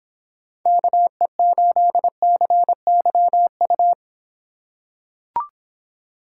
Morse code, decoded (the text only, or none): KE8CYU